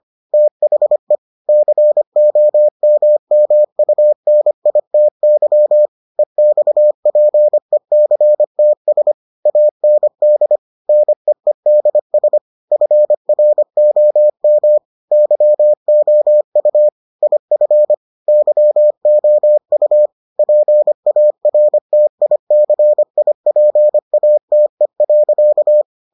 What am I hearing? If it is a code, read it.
Morse code, heard as a keyed tone, about 25 wpm, 600 Hz. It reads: THE COMMUNITY EXPECTS AND NEEDS FROM YOU IF YOU PARTICIPATE.